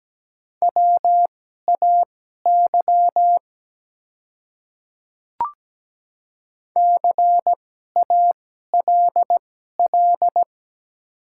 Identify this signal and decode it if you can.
Morse code, heard as a keyed tone, about 17 wpm, spaced as for 12 wpm, 690 Hz. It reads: WAY CALL
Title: 2x Character Spacing - Top 100 Words - 17wpm